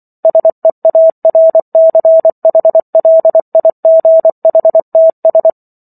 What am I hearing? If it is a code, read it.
Morse code, heard as a keyed tone, about 24 wpm, 650 Hz. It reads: SEARCHLIGHTS